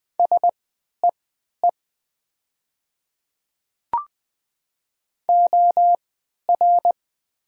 Morse code, heard as a keyed tone, about 20 wpm, spaced as for 11 wpm, 700 Hz. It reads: SEE OR
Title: 3x Character Spacing - Top 100 Words - 20wpm